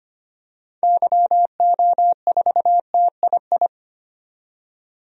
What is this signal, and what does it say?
Morse code, heard as a keyed tone, about 25 wpm, 700 Hz. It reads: YO4TII
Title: DXCC-100 Call Signs - 6 Characters - 25wpm